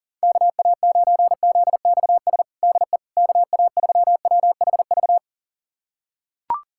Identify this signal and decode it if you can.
Morse code, heard as a keyed tone, about 40 wpm, 700 Hz. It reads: KA9ZXS DE KA3WHV
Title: USA Callsign DE Callsign - 6 Characters - 40wpm